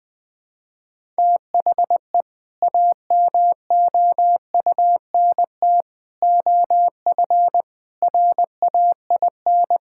Morse code, heard as a keyed tone, about 20 wpm, 700 Hz. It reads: THE AMOUNT OF RAIN
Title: Sentences with 3-Word Phrases - 20wpm